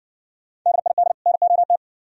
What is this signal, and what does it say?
Morse code, heard as a keyed tone, about 60 wpm, 700 Hz. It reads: DID NOT